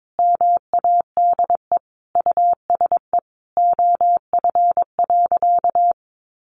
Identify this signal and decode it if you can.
Morse code, heard as a keyed tone, about 22 wpm, 700 Hz. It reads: MADE USE OF.
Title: Sentences from Top 1200 Words 22wpm